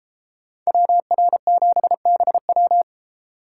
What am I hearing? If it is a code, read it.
Morse code, heard as a keyed tone, about 33 wpm, 700 Hz. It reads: WR7BW